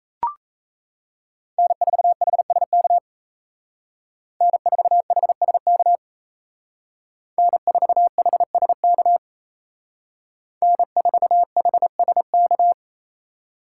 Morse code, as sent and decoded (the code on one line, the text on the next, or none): -. ....- .... ... -.- / -. ....- .... ... -.- / -. ....- .... ... -.- / -. ....- .... ... -.-
N4HSK N4HSK N4HSK N4HSK